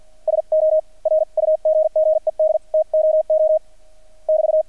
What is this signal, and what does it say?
Morse code, heard as a keyed tone, about 50 wpm, spaced as for 43 wpm, 630 Hz. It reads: SO RUGGED TOO <BT>
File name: archived / CRSnet